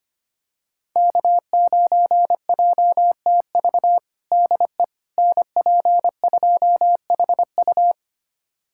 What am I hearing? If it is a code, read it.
Morse code, heard as a keyed tone, about 25 wpm, 700 Hz. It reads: K9JTV DE NP2HU
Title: USA Callsign DE Callsign - 5 Characters - 25wpm